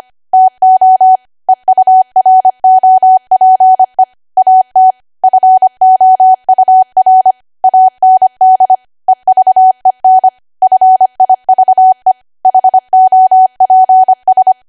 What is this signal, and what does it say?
Morse code, heard as a keyed tone, about 25 wpm, 750 Hz. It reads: TO EUROPE AT FOUR AND EVEN FIVE HOPS
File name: cwa / W1AW-Level3 / 25WPM